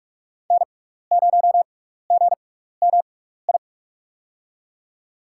Morse code, heard as a keyed tone, about 44 wpm, spaced as for 15 wpm, 700 Hz. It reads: N0GMI
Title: USA Call Signs - 5 Characters - 6x Character Spacing - 45wpm